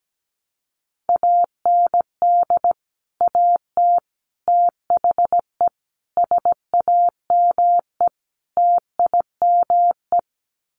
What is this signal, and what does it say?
Morse code, heard as a keyed tone, about 17 wpm, 700 Hz. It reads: AND AT THE SAME TIME
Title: Sets of 5 Words 17wpm